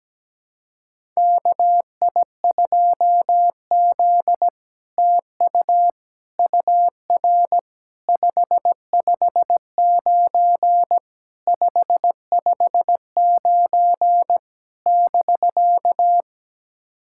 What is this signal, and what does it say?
Morse code, heard as a keyed tone, about 17 wpm, 700 Hz. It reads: KI2Z TU UR 559 559 <BK>
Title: SOTA QSO Practice 17wpm